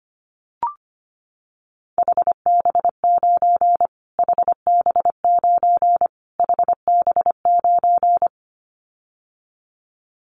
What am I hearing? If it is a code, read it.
Morse code, heard as a keyed tone, about 25 wpm, 700 Hz. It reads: HB9 HB9 HB9
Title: Callsign Prefixes for Top 25 DXCC - Repeated 3x - 25wpm